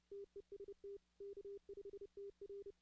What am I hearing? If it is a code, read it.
Morse code, heard as a keyed tone, about 30 wpm, 400 Hz. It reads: TEST K5TR
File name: 2016-ARRL-DX-CW